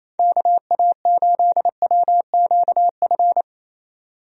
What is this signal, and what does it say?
Morse code, heard as a keyed tone, about 28 wpm, 700 Hz. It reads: KA8WQF
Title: USA Call Signs - 6 Characters - 28wpm